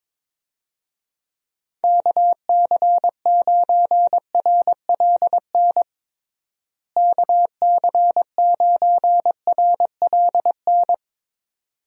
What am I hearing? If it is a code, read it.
Morse code, heard as a keyed tone, about 22 wpm, 700 Hz. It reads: KC9RLN KC9RLN
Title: USA Call Signs - 6 Characters - Repeated 2x - 22wpm